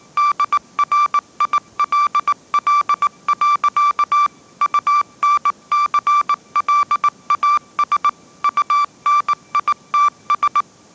Morse code, heard as a keyed tone, about 18 wpm, 1170 Hz. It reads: DRILL. UNCLAS UNITS
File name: 20wpm